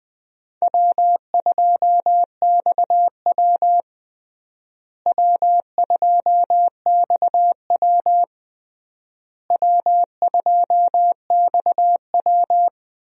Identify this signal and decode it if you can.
Morse code, heard as a keyed tone, about 20 wpm, 700 Hz. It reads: W2XW W2XW W2XW
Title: USA Call Signs - 4 Characters - Repeated 3x - 20wpm